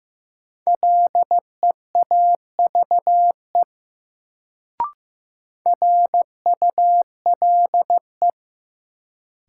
Morse code, heard as a keyed tone, about 15 wpm, 700 Hz. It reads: LEAVE RULE
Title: Sentences from Top 400 Words - New Word Drill - 15wpm